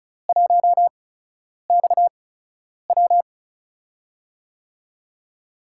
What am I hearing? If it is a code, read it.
Morse code, heard as a keyed tone, about 36 wpm, spaced as for 10 wpm, 700 Hz. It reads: 1XW
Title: USA Call Signs - 8x Character Spacing - 35wpm